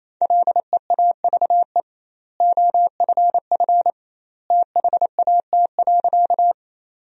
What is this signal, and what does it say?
Morse code, heard as a keyed tone, about 28 wpm, 700 Hz. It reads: LEAVE OFF THAT.
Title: 2x Word-Spacing - Sentences from Top 400 Words - 28wpm